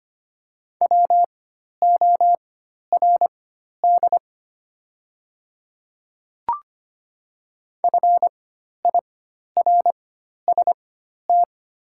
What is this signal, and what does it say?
Morse code, heard as a keyed tone, about 25 wpm, spaced as for 12 wpm, 690 Hz. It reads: WORD FIRST
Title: 4x Character Spacing - Top 500 Words - 25wpm